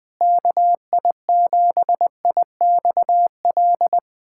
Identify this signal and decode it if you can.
Morse code, heard as a keyed tone, about 20 wpm, 700 Hz. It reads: KI7IXL
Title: USA Callsign DE Callsign - 6 Characters - 20wpm